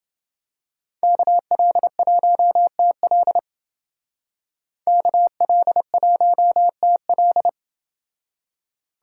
Morse code, heard as a keyed tone, about 28 wpm, 700 Hz. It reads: KL1TL KL1TL